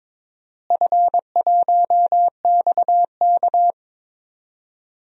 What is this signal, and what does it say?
Morse code, heard as a keyed tone, about 22 wpm, 700 Hz. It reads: F1XK
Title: DXCC-100 Call Signs - 4 Characters - Repeated 2x - 22wpm